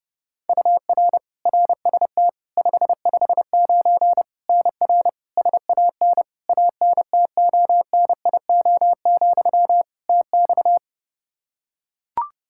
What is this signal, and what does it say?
Morse code, heard as a keyed tone, about 30 wpm, 700 Hz. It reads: UR RST 559 NR SAN ANTONIO, TX
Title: K1USN SST QSO Practice 30wpm